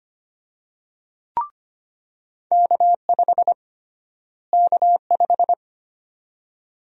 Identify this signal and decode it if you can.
Morse code, heard as a keyed tone, about 25 wpm, 700 Hz. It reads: K5 K5
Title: USA Call Sign Prefixes - Repeated 2x - 25wpm